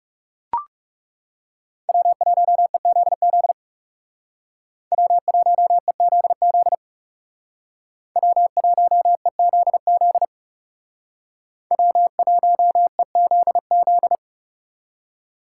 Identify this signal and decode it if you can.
Morse code, heard as a keyed tone, about 36 wpm, 700 Hz. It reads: W1EZZ W1EZZ W1EZZ W1EZZ
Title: Speed Racing - USA 1x3 Call Signs 30wpm